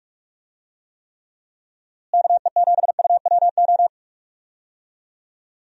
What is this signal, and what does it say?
Morse code, heard as a keyed tone, about 48 wpm, 700 Hz. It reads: KE7UWO
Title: USA Call Signs - 6 Characters - Repeated 3x - 45wpm